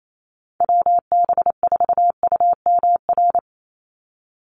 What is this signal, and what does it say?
Morse code, heard as a keyed tone, about 28 wpm, 700 Hz. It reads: WB4UMR
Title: USA Call Signs 28wpm